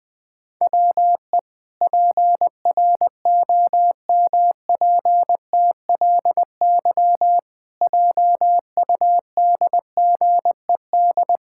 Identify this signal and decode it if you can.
Morse code, heard as a keyed tone, about 20 wpm, 700 Hz. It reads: WE PROMPTLY JUDGED
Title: Pangram Challenge v2 - 20wpm